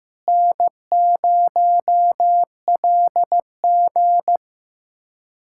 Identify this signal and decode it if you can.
Morse code, heard as a keyed tone, about 15 wpm, 700 Hz. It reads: N0LG
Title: USA Callsign DE Callsign - 4 Characters - 15wpm